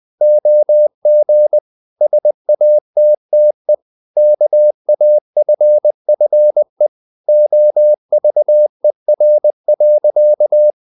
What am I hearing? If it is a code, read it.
Morse code, heard as a keyed tone, about 20 wpm, 600 Hz. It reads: OG SATTE KAFFE OVER.
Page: Ferie 19 wpm